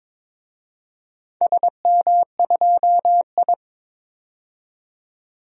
Morse code, heard as a keyed tone, about 22 wpm, 700 Hz. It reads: SM2I
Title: DXCC-100 Call Signs - 4 Characters - 22wpm